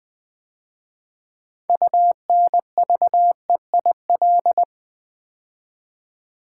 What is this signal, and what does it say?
Morse code, heard as a keyed tone, about 20 wpm, 700 Hz. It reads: UNVEIL
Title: 6 Letter Words - First 3 Letters Given - 20wpm